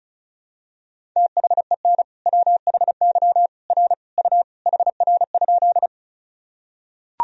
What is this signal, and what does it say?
Morse code, heard as a keyed tone, about 36 wpm, 700 Hz. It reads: THEN WHY R U HR?
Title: Sentences from Top 200 Words and Abbreviations 35wpm